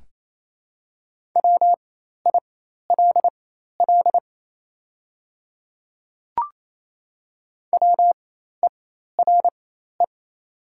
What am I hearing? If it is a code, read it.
Morse code, heard as a keyed tone, about 28 wpm, spaced as for 13 wpm, 700 Hz. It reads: WILL WERE
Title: 4x Character Spacing - Top 100 Words - 28wpm